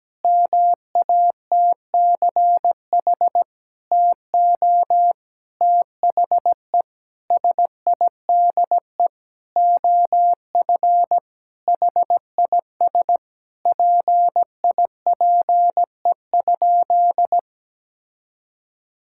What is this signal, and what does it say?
Morse code, heard as a keyed tone, about 17 wpm, 700 Hz. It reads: MATCH TO THE SIDE OF HIS PIPE?
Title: Sentences from Top 1600 Words 17wpm